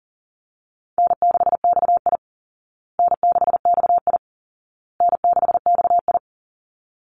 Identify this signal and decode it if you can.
Morse code, heard as a keyed tone, about 40 wpm, 700 Hz. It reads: N6XI N6XI N6XI
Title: Call Signs Repeated 2x - 40wpm